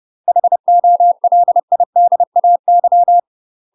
Morse code, heard as a keyed tone, about 30 wpm, 700 Hz. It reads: HOLIDAY
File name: UK MASTER QSOs TRAINING / 30WPM / UK MASTER QSOs 0052